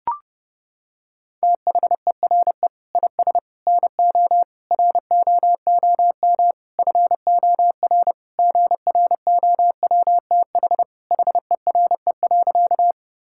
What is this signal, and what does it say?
Morse code, heard as a keyed tone, about 31 wpm, 700 Hz. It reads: THERE IS NO ROOM FOR GROWTH HERE.
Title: Sentences from Top 1000 Words - Review Part 5 - 30wpm